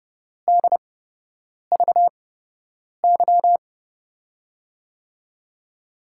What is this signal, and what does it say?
Morse code, heard as a keyed tone, about 30 wpm, spaced as for 8 wpm, 700 Hz. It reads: DVY
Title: USA Call Signs - 8x Character Spacing - 30wpm